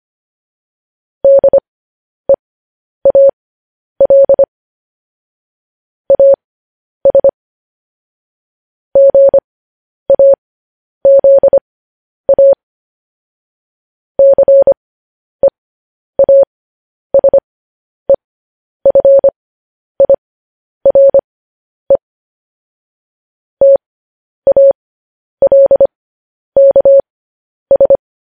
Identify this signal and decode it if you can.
Morse code, heard as a keyed tone, about 25 wpm, spaced as for 10 wpm, 550 Hz. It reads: DEAL AS GAZA CEASEFIRE TALKS
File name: podcast / News headlines-1x-25x10